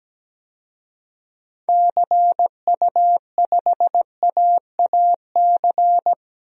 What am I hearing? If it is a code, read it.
Morse code, heard as a keyed tone, about 17 wpm, 700 Hz. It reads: CU5AAC
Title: DXCC-100 Call Signs - 6 Characters - 17wpm